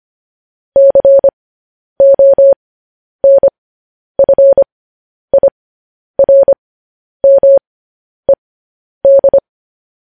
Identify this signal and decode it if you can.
Morse code, heard as a keyed tone, about 25 wpm, spaced as for 10 wpm, 550 Hz. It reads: CONFIRMED